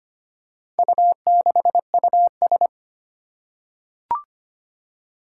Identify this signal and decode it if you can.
Morse code, heard as a keyed tone, about 25 wpm, 700 Hz. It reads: U6US
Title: Four Challenging Letter-Number Combinations 25wpm